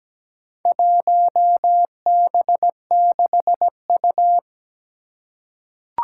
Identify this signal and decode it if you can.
Morse code, heard as a keyed tone, about 17 wpm, 700 Hz. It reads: 1B6U
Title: Four Challenging Letter-Number Combinations 17wpm